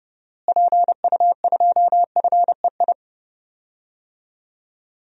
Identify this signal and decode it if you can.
Morse code, heard as a keyed tone, about 32 wpm, 700 Hz. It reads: PU2FEI
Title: DXCC-100 Call Signs - 6 Characters - Repeated 3x - 30wpm